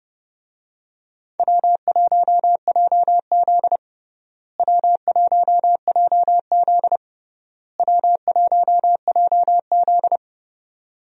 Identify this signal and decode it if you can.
Morse code, heard as a keyed tone, about 30 wpm, 700 Hz. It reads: W1JZ W1JZ W1JZ
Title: USA Call Signs - 4 Characters - Repeated 3x - 30wpm